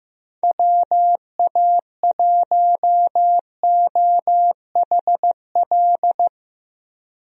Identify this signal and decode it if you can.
Morse code, heard as a keyed tone, about 15 wpm, 700 Hz. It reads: WA1OHL
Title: USA Call Signs - 6 Characters - Repeated 2x - 15wpm